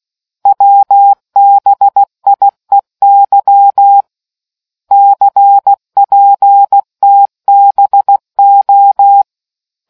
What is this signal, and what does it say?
Morse code, heard as a keyed tone, about 16 wpm, 800 Hz. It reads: WBIEY CPTBO